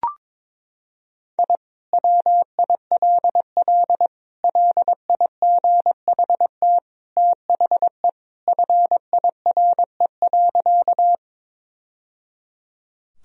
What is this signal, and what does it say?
Morse code, heard as a keyed tone, about 22 wpm, 700 Hz. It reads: I WILL LIGHT THE FIRE.
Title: Sentences from Top 1000 Words - Review Part 1 - 22wpm